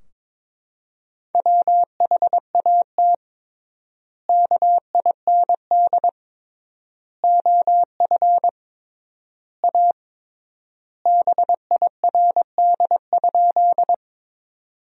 Morse code, heard as a keyed tone, about 22 wpm, 700 Hz. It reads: WHAT KIND OF A BIRD?
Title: 3x Word-Spacing - Sentences from Top 400 Words - 22wpm